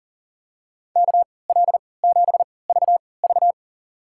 Morse code, heard as a keyed tone, about 40 wpm, spaced as for 23 wpm, 700 Hz. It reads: KL7VV
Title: USA Call Signs - 5 Characters - 3x Character Spacing - 40wpm